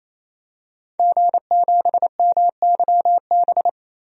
Morse code, heard as a keyed tone, about 28 wpm, 700 Hz. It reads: G7MYB
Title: DXCC-100 Call Signs - 5 Characters - 28wpm